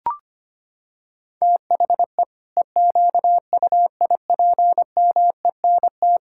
Morse code, heard as a keyed tone, about 25 wpm, 700 Hz. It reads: THE EQUIPMENT
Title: Sentences from Top 1000 Words - Review Part 6 - 25wpm